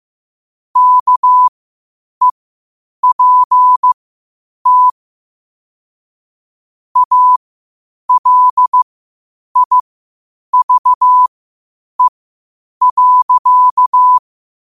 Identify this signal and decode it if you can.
Morse code, heard as a keyed tone, about 15 wpm, spaced as for 8 wpm, 1000 Hz. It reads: KEPT ALIVE.
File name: qotdfiles / 8wpm